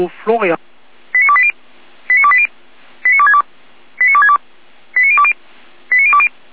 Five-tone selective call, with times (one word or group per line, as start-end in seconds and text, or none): 1.1-1.5 89280
2.1-2.5 89280
3.0-3.4 89262
4.0-4.4 89262
5.0-5.3 89020
5.9-6.3 89020